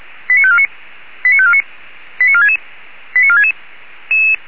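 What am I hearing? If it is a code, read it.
Five-tone selective call (ZVEI), heard as quick runs of five tones, tones of about 70 ms. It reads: 89549 89549 89466 89466
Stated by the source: ZVEI five-tone selective call received by SDR.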